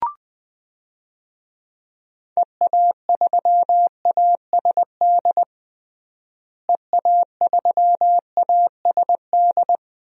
Morse code, heard as a keyed tone, about 21 wpm, 700 Hz. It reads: EA3ASD EA3ASD
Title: DXCC-100 Call Signs - 6 Characters - Repeated 3x - 20wpm